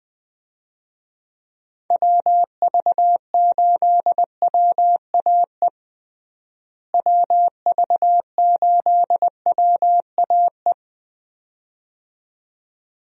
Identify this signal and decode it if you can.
Morse code, heard as a keyed tone, about 21 wpm, 700 Hz. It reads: WV8WAE WV8WAE
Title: USA Call Signs - 6 Characters - Repeated 2x - 20wpm